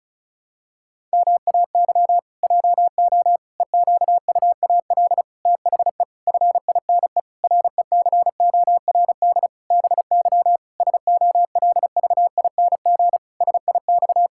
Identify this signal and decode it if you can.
Morse code, heard as a keyed tone, about 36 wpm, 700 Hz. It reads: MAY JO EQUAL THE FINE RECORD BY SOLVING SIX
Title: Pangram Challenge v2 - 35wpm